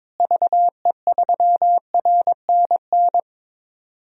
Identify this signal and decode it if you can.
Morse code, heard as a keyed tone, about 22 wpm, 700 Hz. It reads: VE3RNN
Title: DXCC-100 Call Signs - 6 Characters - Repeated 2x - 22wpm